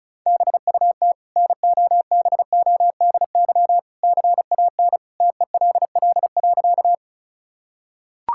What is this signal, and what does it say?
Morse code, heard as a keyed tone, about 36 wpm, 700 Hz. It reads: BUT NOBODY CAN TELL.
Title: Sentences from Top 1600 Words 35wpm